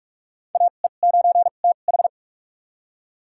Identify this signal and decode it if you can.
Morse code, heard as a keyed tone, about 46 wpm, spaced as for 32 wpm, 700 Hz. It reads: AE9TH
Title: USA Call Signs - 2x Character Spacing - 45wpm